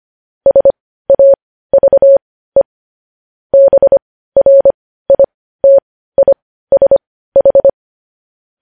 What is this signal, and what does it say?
Morse code, heard as a keyed tone, about 25 wpm, spaced as for 15 wpm, 550 Hz. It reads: SAVE BRITISH
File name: podcast / News headlines-1x-25x15